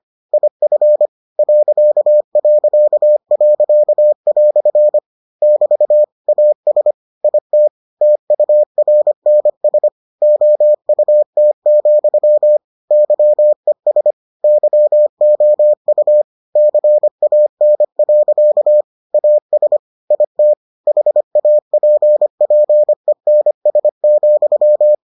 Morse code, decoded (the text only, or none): IF ..." <BT> AS IT TURNS OUT, YES YOU CAN. AS IT HAPPENS,